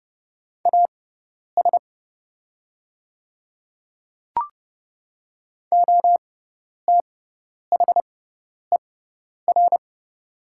Morse code, heard as a keyed tone, about 30 wpm, spaced as for 10 wpm, 700 Hz. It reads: AS OTHER
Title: 6x Character Spacing - Top 100 Words - 30wpm